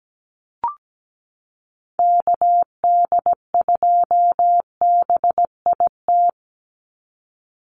Morse code, heard as a keyed tone, about 17 wpm, 700 Hz. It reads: KD2BIT
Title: USA Call Signs 17wpm